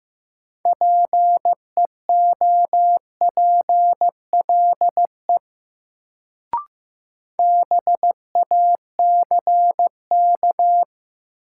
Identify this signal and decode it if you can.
Morse code, heard as a keyed tone, about 15 wpm, 700 Hz. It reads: PEOPLE BACK
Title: Sets of 2 Words - New Word Drill - 15wpm